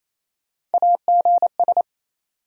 Morse code, heard as a keyed tone, about 28 wpm, 700 Hz. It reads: AGS
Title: USA Call Sign Postfixes - 3 Letters 28wpm